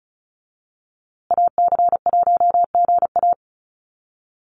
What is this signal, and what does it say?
Morse code, heard as a keyed tone, about 34 wpm, 700 Hz. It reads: AC1GA
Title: USA Call Signs Repeated 2x 35wpm